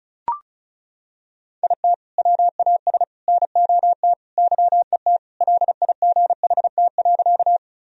Morse code, heard as a keyed tone, about 36 wpm, 700 Hz. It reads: IT WAS NOT YET LIGHT.
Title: Sentences from Top 1000 Words - Review Part 1 - 35wpm